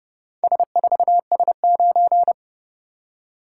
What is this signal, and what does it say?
Morse code, heard as a keyed tone, about 29 wpm, 700 Hz. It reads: S4S9